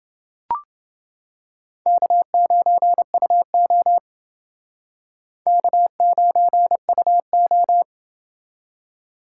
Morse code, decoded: K9UO K9UO

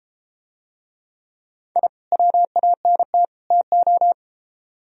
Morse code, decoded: I WANT TO

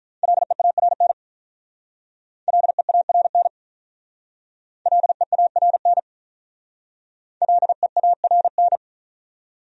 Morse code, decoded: LEARN LEARN LEARN LEARN

